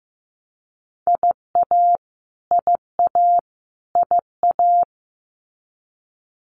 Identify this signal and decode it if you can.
Morse code, heard as a keyed tone, about 15 wpm, 700 Hz. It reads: IA IA IA
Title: US States Repeated 3x 15wpm